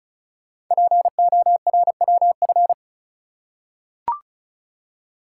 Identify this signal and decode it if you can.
Morse code, heard as a keyed tone, about 36 wpm, 690 Hz. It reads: PORWF
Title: Code Groups - 5 Letters - 35wpm